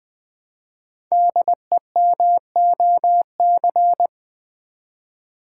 Morse code, heard as a keyed tone, about 20 wpm, 700 Hz. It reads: DEMOC
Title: Build A Word - 5 Letters At A Time - 10-Letter Words - 20wpm